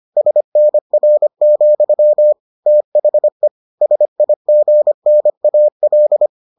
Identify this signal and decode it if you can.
Morse code, heard as a keyed tone, about 25 wpm, 600 Hz. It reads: SNR, THE SIGNAL